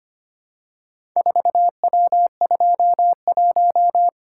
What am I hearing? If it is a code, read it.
Morse code, heard as a keyed tone, about 25 wpm, 690 Hz. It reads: 4W21
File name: practice / 25wpm / Four Letter-Number Combinations - 25wpm